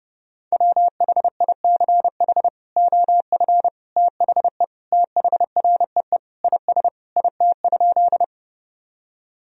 Morse code, heard as a keyed tone, about 32 wpm, 700 Hz. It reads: WHICH OF THE THREE IS IT?
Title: Sentences from Top 1000 Words - Review Part 1 - 30wpm